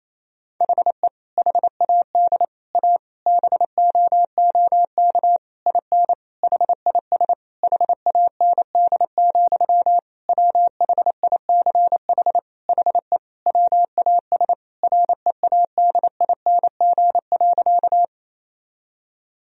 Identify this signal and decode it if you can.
Morse code, heard as a keyed tone, about 28 wpm, 700 Hz. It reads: HE HAD A BOOK IN HIS HAND, WHICH HE WAS READING.
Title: Sentences from Top 1000 Words - Review Part 3 - 28wpm